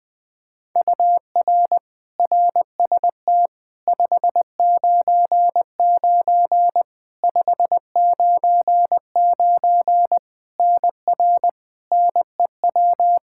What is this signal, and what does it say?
Morse code, heard as a keyed tone, about 21 wpm, 700 Hz. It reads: UR RST 599 599 NR NEW
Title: Practice QSO - Exchange 1 - Beginning - 20wpm